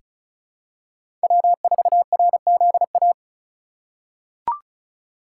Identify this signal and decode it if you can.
Morse code, heard as a keyed tone, about 36 wpm, 700 Hz. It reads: W4RZA
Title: USA Call Signs - 5 Characters - 4x Character Spacing - 35wpm